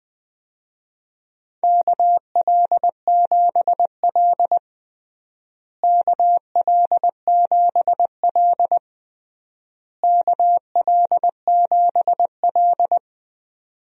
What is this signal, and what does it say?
Morse code, heard as a keyed tone, about 21 wpm, 700 Hz. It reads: KL7L KL7L KL7L
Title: DXCC-100 Call Signs - 4 Characters - Repeated 3x - 20wpm